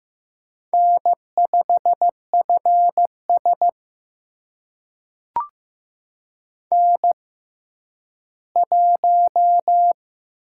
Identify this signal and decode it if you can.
Morse code, heard as a keyed tone, about 15 wpm, spaced as for 10 wpm, 700 Hz. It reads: N5FS N1
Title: USA Call Signs - 4 Characters - 6x Character Spacing - 15wpm